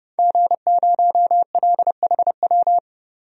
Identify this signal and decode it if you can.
Morse code, heard as a keyed tone, about 31 wpm, 700 Hz. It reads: G0LHW
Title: DXCC-100 Call Signs - 5 Characters - 30wpm